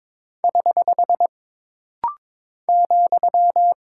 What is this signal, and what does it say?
Morse code, heard as a keyed tone, about 22 wpm, 700 Hz. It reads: <HH> ,
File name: practice / 22wpm / Single Punctuation-Prosign - Rapid-Fire - 22wpm